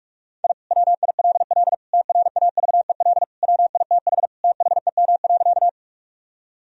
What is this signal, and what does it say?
Morse code, heard as a keyed tone, about 41 wpm, 700 Hz. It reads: I WILL TRAVEL WITH THEM.
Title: Sentences from Top 1000 Words - Review Part 2 - 45wpm